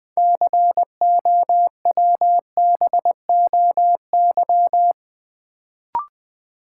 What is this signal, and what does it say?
Morse code, heard as a keyed tone, about 20 wpm, 700 Hz. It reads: COWBOY
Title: Build A Word - 3 Letters At A Time - 6-Letter Words - 20wpm